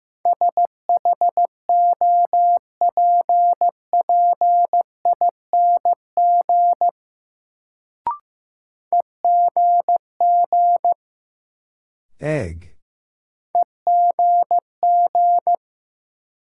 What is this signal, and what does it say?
Morse code, heard as a keyed tone, about 15 wpm, 700 Hz. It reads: SHOPPING EGG EGG
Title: Sentences from Top 1000 Words - Review Part 7 - New Word Drill - 15wpm